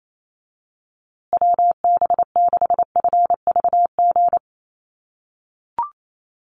Morse code, heard as a keyed tone, about 28 wpm, 700 Hz. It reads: WB6FVG
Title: USA Call Signs Repeated 3x 28wpm